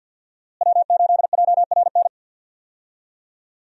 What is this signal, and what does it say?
Morse code, heard as a keyed tone, about 53 wpm, 700 Hz. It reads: W8JRN
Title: USA Call Signs - 5 Characters - Repeated 3x - 50wpm